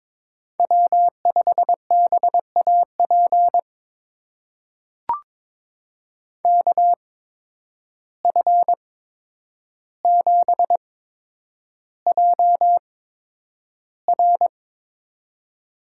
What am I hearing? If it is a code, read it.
Morse code, heard as a keyed tone, about 22 wpm, spaced as for 9 wpm, 700 Hz. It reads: W5BAP KF7JR
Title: USA Call Signs - 5 Characters - 8x Character Spacing - 22wpm